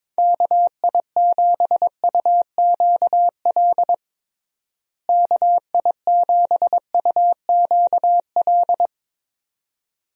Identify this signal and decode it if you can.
Morse code, heard as a keyed tone, about 22 wpm, 700 Hz. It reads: KI7UQL KI7UQL
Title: USA Call Signs - 6 Characters - Repeated 2x - 22wpm